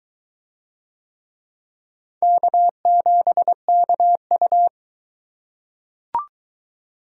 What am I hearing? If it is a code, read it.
Morse code, heard as a keyed tone, about 22 wpm, 700 Hz. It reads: K7KU